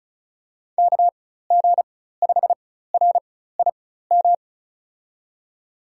Morse code, heard as a keyed tone, about 36 wpm, spaced as for 16 wpm, 700 Hz. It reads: KG5RIM
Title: USA Call Signs - 6 Characters - 4x Character Spacing - 35wpm